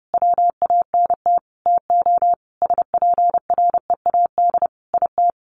Morse code, heard as a keyed tone, about 30 wpm, 700 Hz. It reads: WANT TO SPREAD IT
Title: Sentences from Top 1200 Words 30wpm